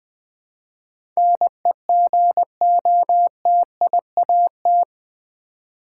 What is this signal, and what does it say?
Morse code, heard as a keyed tone, about 20 wpm, 700 Hz. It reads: NEGOTIAT
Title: Build A Word - 4 Letters At A Time - 11-Letter Words - 20wpm